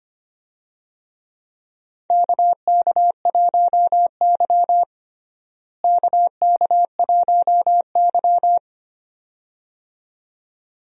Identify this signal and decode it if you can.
Morse code, heard as a keyed tone, about 25 wpm, 700 Hz. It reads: KK1Y KK1Y
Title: USA Call Signs - 4 Characters - Repeated 2x - 25wpm